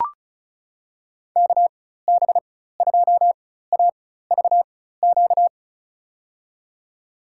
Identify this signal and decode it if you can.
Morse code, heard as a keyed tone, about 36 wpm, spaced as for 16 wpm, 700 Hz. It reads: KB2AVQ
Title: USA Call Signs - 6 Characters - 4x Character Spacing - 35wpm